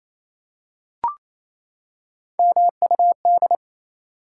Morse code, heard as a keyed tone, about 27 wpm, 700 Hz. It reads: MUD
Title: Three Letter Words - 28wpm